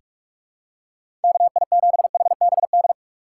Morse code, heard as a keyed tone, about 42 wpm, 700 Hz. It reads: KI7HBD
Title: USA Call Signs - 6 Characters - 3x Character Spacing - 45wpm